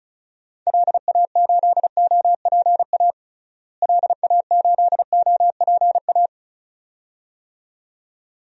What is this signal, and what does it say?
Morse code, heard as a keyed tone, about 36 wpm, 700 Hz. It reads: LA8OPA LA8OPA
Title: DXCC-100 Call Signs - 6 Characters - Repeated 2x - 35wpm